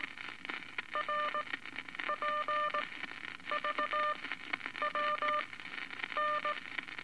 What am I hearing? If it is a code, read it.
Morse code, heard as a keyed tone, about 18 wpm, spaced as for 9 wpm, 1200 Hz, with radio noise. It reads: RPVWN
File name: Eight Words Per Minute